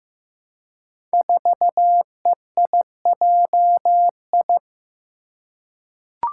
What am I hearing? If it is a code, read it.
Morse code, heard as a keyed tone, about 15 wpm, 700 Hz. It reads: 4EIJI
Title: Five Challenging Letter-Number Combinations 15wpm